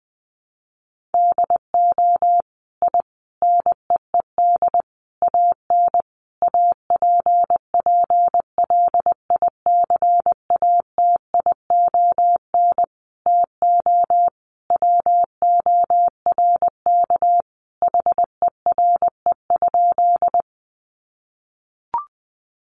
Morse code, heard as a keyed tone, about 20 wpm, 700 Hz. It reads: DO I NEED AN APPLICATION TO WORK HERE?
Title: Sentences from Top 1200 Words 20wpm